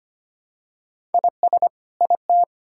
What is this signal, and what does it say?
Morse code, heard as a keyed tone, about 25 wpm, 700 Hz. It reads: IS IT
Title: Sentences from Top 1000 Words - Review Part 6 - 25wpm